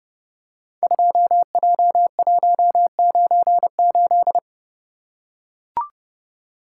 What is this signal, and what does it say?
Morse code, heard as a keyed tone, about 32 wpm, 700 Hz. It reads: 2J198
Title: Code Groups - 5 Letter:Numbers - 30wpm